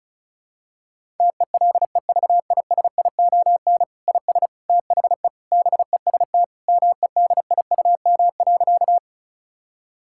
Morse code, heard as a keyed tone, about 34 wpm, 700 Hz. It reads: TELEVISION IS THE BEST MEDIUM.
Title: Sentences from Top 1000 Words - Review Part 6 - 35wpm